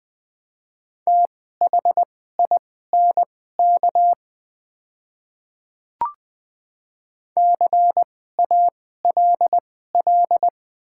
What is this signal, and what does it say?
Morse code, heard as a keyed tone, about 20 wpm, spaced as for 14 wpm, 700 Hz. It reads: THINK CALL
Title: 2x Character Spacing - Top 200 Words - 20wpm